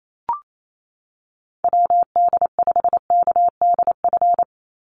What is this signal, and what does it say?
Morse code, heard as a keyed tone, about 28 wpm, 700 Hz. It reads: WD5KDF
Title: USA Call Signs 28wpm